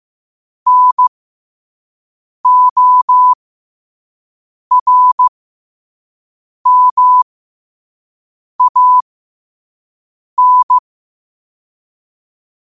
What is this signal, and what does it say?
Morse code, heard as a keyed tone, about 15 wpm, spaced as for 5 wpm, 1000 Hz. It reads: NORMAN
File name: qotdfiles / 5wpm